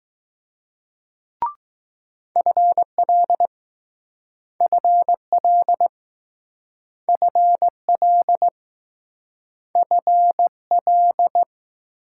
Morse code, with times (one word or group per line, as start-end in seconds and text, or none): 2.4-3.5 FL
4.6-5.9 FL
7.1-8.5 FL
9.8-10.5 F
10.7-11.4 L